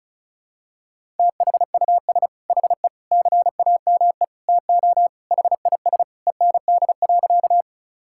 Morse code, heard as a keyed tone, about 36 wpm, 700 Hz. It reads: THUS HE CAME TO HIS END.
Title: Sentences from Top 1000 Words - Review Part 5 - 35wpm